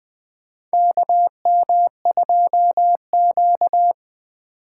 Morse code, heard as a keyed tone, about 20 wpm, 700 Hz. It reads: KM2Q
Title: USA Call Signs - 6x Character Spacing - 20wpm